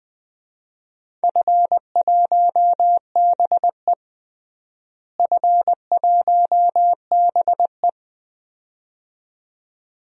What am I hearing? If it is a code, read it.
Morse code, heard as a keyed tone, about 20 wpm, 700 Hz. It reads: F1BE F1BE
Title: DXCC-100 Call Signs - 4 Characters - Repeated 2x - 20wpm